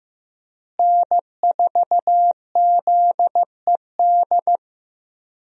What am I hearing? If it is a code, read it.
Morse code, heard as a keyed tone, about 15 wpm, 700 Hz. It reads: N4ZED
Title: USA Call Signs - 5 Characters - Repeated 3x - 15wpm